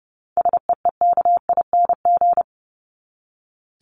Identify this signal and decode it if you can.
Morse code, heard as a keyed tone, about 30 wpm, 700 Hz. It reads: SEEKING
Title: ING Suffix - Top 100 Words with Base Words 30wpm